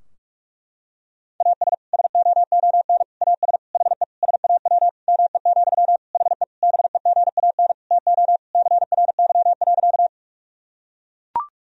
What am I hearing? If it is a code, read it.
Morse code, heard as a keyed tone, about 41 wpm, 700 Hz. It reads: AS SOON AS HE SAW ME, HE BEGAN TO CRY.
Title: Sentences from Top 1000 Words - Review Part 2 - 45wpm